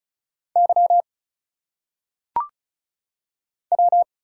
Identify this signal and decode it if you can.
Morse code, heard as a keyed tone, about 36 wpm, 700 Hz. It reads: Y W
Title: Single Letters - Phonetic Pronunciation 35wpm